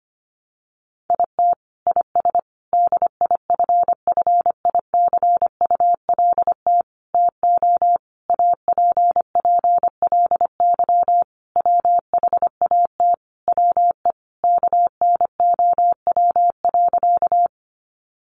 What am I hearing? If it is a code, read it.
Morse code, heard as a keyed tone, about 25 wpm, 700 Hz. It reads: IT IS DIFFICULT TO APPLY WHAT WE KNOW.
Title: Sentences from Top 1200 Words 25wpm